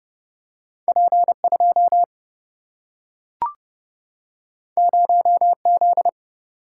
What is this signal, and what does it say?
Morse code, heard as a keyed tone, about 31 wpm, 700 Hz. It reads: P 2 0 Z
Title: Single-Letter Single-Number 30wpm